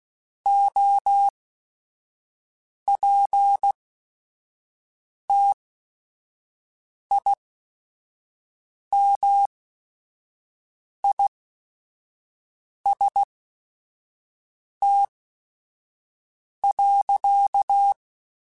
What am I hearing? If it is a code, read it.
Morse code, heard as a keyed tone, about 16 wpm, spaced as for 5 wpm, 800 Hz. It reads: OPTIMIST.
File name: Quotes / Quotes-mp3s / Optimist @5